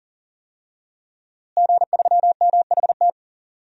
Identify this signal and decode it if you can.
Morse code, heard as a keyed tone, about 40 wpm, 700 Hz. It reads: G3MHT
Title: DXCC-100 Call Signs - 5 Characters - Repeated 3x - 40wpm